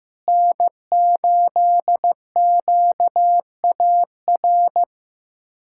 Morse code, heard as a keyed tone, about 16 wpm, 700 Hz. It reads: N8QAR
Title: K1USN SST QSO Practice - 15wpm